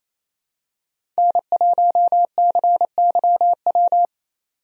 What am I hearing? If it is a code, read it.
Morse code, heard as a keyed tone, about 28 wpm, 700 Hz. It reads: N1CYW
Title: USA Call Signs - 5 Characters - 28wpm